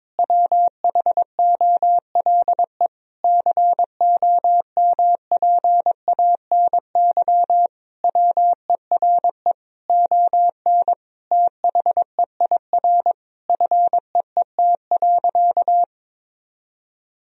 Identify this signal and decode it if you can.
Morse code, heard as a keyed tone, about 22 wpm, 700 Hz. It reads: WHOLE COMPANY WERE ON THEIR FEET.
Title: Sentences from Top 1000 Words - Review Part 3 - 22wpm